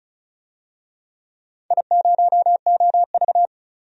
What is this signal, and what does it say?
Morse code, heard as a keyed tone, about 36 wpm, 690 Hz. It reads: I0OV
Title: DXCC-100 Call Signs - 4 Characters - 35wpm